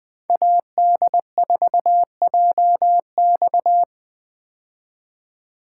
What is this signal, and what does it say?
Morse code, heard as a keyed tone, about 20 wpm, 700 Hz. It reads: AD4JX
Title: 2023 POTA Hunter Call Signs - 20wpm